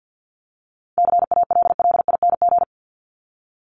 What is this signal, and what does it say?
Morse code, heard as a keyed tone, about 50 wpm, 700 Hz. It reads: CALLING
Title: ING Suffix - Top 100 Words with Base Words 50wpm